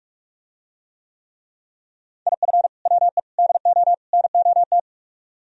Morse code, heard as a keyed tone, about 46 wpm, 700 Hz. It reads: IF WE DO NOT